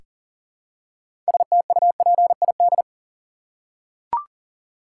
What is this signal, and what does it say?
Morse code, heard as a keyed tone, about 40 wpm, 700 Hz. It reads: STUPID
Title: Six Letter Words - 40wpm